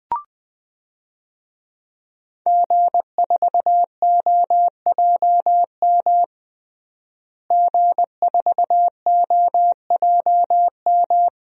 Morse code, heard as a keyed tone, about 20 wpm, 700 Hz. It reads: G4OJM G4OJM
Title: DXCC-100 Call Signs - 5 Characters - Repeated 3x - 20wpm